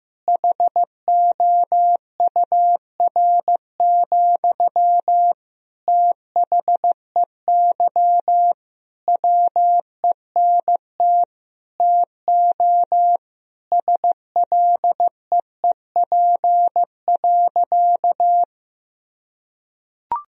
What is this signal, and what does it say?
Morse code, heard as a keyed tone, about 16 wpm, 700 Hz. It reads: HOUR, THEY WENT TO SLEEP.
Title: Sentences from Top 1000 Words - Review Part 2 - 15wpm